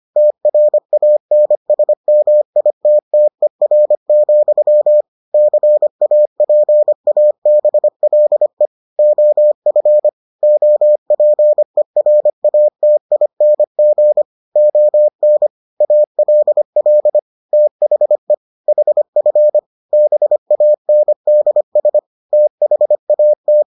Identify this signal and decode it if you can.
Morse code, heard as a keyed tone, about 25 wpm, 600 Hz. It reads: TRANSMITTER, CAPABLE OF OPERATING ON ALL THE HF BANDS THAT